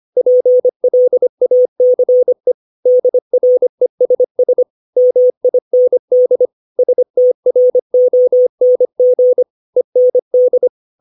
Morse code, heard as a keyed tone, about 25 wpm, 500 Hz. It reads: PLACE DRESS MIND STRONG END
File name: 632 words / 632 Words 25WPM